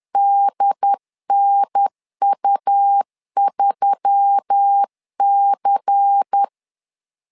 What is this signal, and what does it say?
Morse code, heard as a keyed tone, about 10 wpm, 800 Hz. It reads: DNU3C
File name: LEZIONI / 5 LBVC / 5 LBVC 10wpm 2349